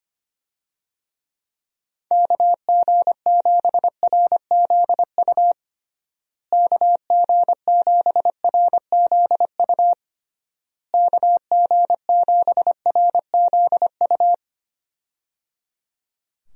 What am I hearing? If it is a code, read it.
Morse code, heard as a keyed tone, about 25 wpm, 700 Hz. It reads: KG7RZU KG7RZU KG7RZU
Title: USA Call Signs - 6 Characters - Repeated 3x - 25wpm